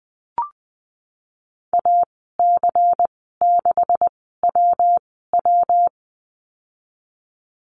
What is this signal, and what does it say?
Morse code, heard as a keyed tone, about 20 wpm, spaced as for 14 wpm, 700 Hz. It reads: AC6WW
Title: 128 Lesson 21g - Number 6 - Call Signs 20wpm